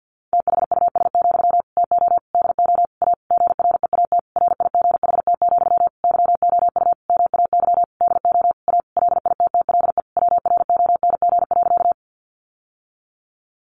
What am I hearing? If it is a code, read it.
Morse code, heard as a keyed tone, about 50 wpm, 700 Hz. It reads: THUS, TO DO A GREAT RIGHT, YOU MAY DO A LITTLE WRONG.
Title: Sentences from Top 900 Words 50wpm